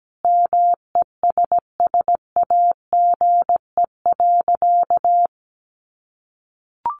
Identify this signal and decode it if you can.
Morse code, heard as a keyed tone, about 17 wpm, 700 Hz. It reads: MESSAGE.
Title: Sentences from Top 1100 Words 17wpm